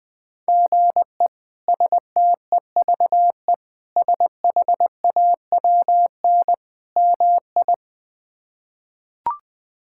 Morse code, heard as a keyed tone, about 20 wpm, 700 Hz. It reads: GE STEVE SHAWN MI
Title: K1USN SST QSO Practice - 20wpm